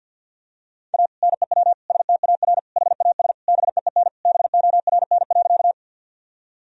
Morse code, heard as a keyed tone, about 52 wpm, 700 Hz. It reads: A NEW STAR HAS BEEN BORN.
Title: Sentences from Top 1000 Words - Review Part 2 - 50wpm